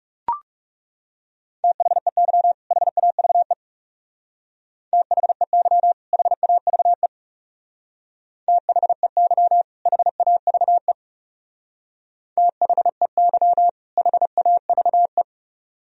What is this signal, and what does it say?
Morse code, heard as a keyed tone, about 38 wpm, 700 Hz. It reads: THEY HAVE THEY HAVE THEY HAVE THEY HAVE